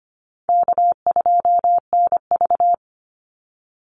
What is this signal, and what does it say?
Morse code, heard as a keyed tone, about 25 wpm, 700 Hz. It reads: K2NV
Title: Easy Call Signs Repeated 2x - 25wpm